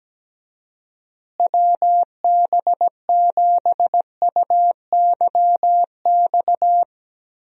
Morse code, heard as a keyed tone, about 17 wpm, 700 Hz. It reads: WB7UYX
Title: USA Call Signs - 6 Characters - Repeated 3x - 17wpm